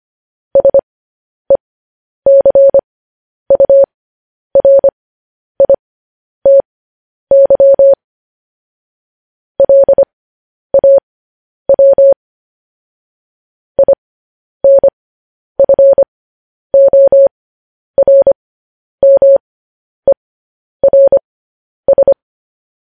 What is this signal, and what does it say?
Morse code, heard as a keyed tone, about 25 wpm, spaced as for 10 wpm, 550 Hz. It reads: SECURITY LAW INFORMERS
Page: News Headlines in Morse Code at 10 WPM